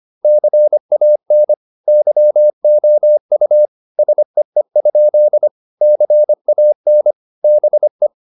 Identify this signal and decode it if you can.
Morse code, heard as a keyed tone, about 25 wpm, 600 Hz. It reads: CAN YOU SEE? CAN BE